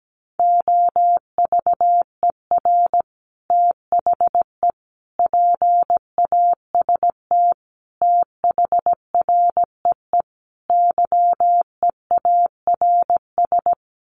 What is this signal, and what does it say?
Morse code, heard as a keyed tone, about 17 wpm, 700 Hz. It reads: OVER THE PAST THREE YEARS
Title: Sets of 5 Words Encore2 17wpm